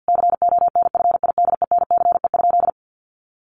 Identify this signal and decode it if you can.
Morse code, heard as a keyed tone, about 50 wpm, 700 Hz. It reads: CONFIDENCE?
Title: Sentences from Top 1500 Words 50wpm